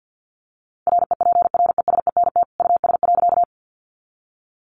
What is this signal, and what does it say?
Morse code, heard as a keyed tone, about 50 wpm, 700 Hz. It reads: REPRESENT US.
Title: Sentences from Top 1200 Words 50wpm